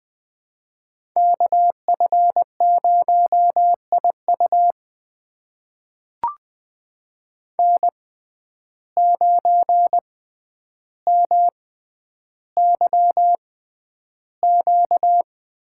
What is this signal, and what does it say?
Morse code, heard as a keyed tone, about 20 wpm, spaced as for 10 wpm, 700 Hz. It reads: KF0IU N9MYQ
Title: USA Call Signs - 5 Characters - 6x Character Spacing - 20wpm